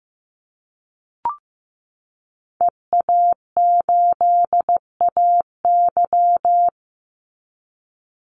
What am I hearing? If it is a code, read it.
Morse code, heard as a keyed tone, about 15 wpm, 700 Hz. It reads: EA8AY
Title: Call Signs Repeated 3x - 15wpm